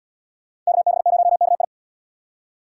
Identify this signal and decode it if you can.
Morse code, heard as a keyed tone, about 65 wpm, 700 Hz. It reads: KB0KI